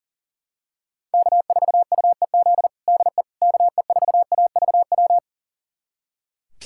Morse code, heard as a keyed tone, about 40 wpm, 700 Hz. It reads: K4UEZ DE KE4AVW
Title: USA Callsign DE Callsign 40wpm